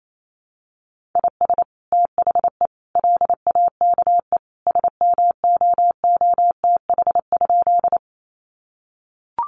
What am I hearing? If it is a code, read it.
Morse code, heard as a keyed tone, about 28 wpm, 700 Hz. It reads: IS THE LAKE SMOOTH?
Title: Sentences from Top 1400 Words 28wpm